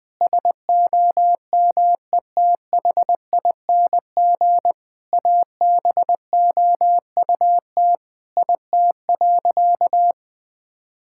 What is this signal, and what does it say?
Morse code, heard as a keyed tone, about 21 wpm, 700 Hz. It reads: SOMETHING ABOUT IT.
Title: Sentences with 5-Word Phrases - 20wpm